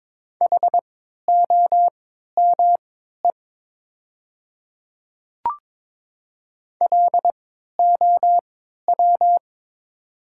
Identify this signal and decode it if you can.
Morse code, heard as a keyed tone, about 22 wpm, spaced as for 12 wpm, 700 Hz. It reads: HOME LOW
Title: 3x Character Spacing - Top 300 Words - 22wpm